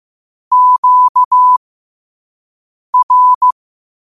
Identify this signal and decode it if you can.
Morse code, heard as a keyed tone, about 15 wpm, spaced as for 5 wpm, 1000 Hz. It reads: QR
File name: qotdfiles / 5wpm